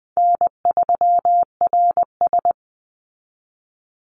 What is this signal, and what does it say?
Morse code, heard as a keyed tone, about 20 wpm, 700 Hz. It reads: N3RS
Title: Easy Call Signs Repeated 2x - 20wpm